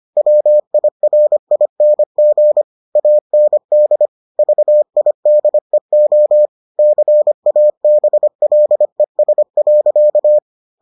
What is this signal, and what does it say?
Morse code, heard as a keyed tone, about 25 wpm, 600 Hz. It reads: WIRING AND VIDEO CABLES.